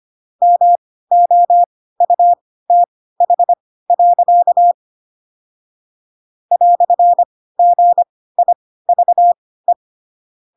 Morse code, heard as a keyed tone, about 25 wpm, spaced as for 16 wpm, 700 Hz. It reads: MOUTH. "GIVE
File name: Pinocchio / Pinocchio ch2 10 25 ik1hge